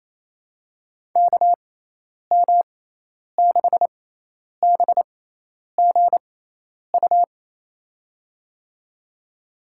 Morse code, heard as a keyed tone, about 28 wpm, spaced as for 10 wpm, 700 Hz. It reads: KM6BGU
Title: USA Call Signs - 6 Characters - 6x Character Spacing - 28wpm